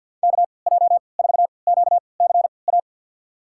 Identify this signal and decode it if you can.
Morse code, heard as a keyed tone, about 46 wpm, spaced as for 28 wpm, 690 Hz. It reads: KJ4QCA
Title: USA Call Signs - 6 Characters - 3x Character Spacing - 50wpm